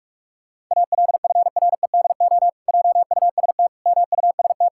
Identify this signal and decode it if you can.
Morse code, heard as a keyed tone, about 46 wpm, 700 Hz. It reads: ALFREDO JUST MUST
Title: Pangram Challenge v2 - 45wpm